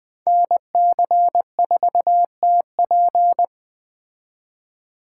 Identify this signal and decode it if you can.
Morse code, heard as a keyed tone, about 20 wpm, 700 Hz. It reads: NC4TP
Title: Speed Racing - USA 2x2 Call Signs 17wpm